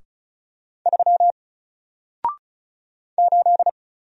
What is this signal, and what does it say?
Morse code, heard as a keyed tone, about 36 wpm, 700 Hz. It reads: 3 8
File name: Single-Letter-Number-35wpm